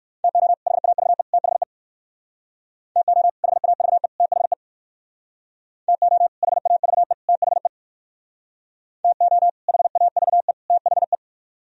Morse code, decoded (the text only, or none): TO HAVE THE TO HAVE THE TO HAVE THE TO HAVE THE